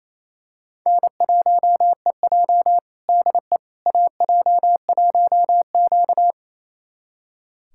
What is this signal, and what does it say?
Morse code, heard as a keyed tone, about 28 wpm, 700 Hz. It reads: N1EJ DE AJ1Q
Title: USA Callsign DE Callsign - 4 Characters - 28wpm